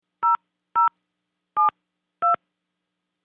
DTMF, heard as telephone touch tones, tones of about 140 ms. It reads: 0072